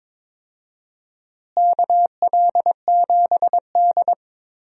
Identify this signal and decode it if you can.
Morse code, heard as a keyed tone, about 22 wpm, 700 Hz. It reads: KL7D
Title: DXCC-100 Call Signs - 4 Characters - Repeated 3x - 22wpm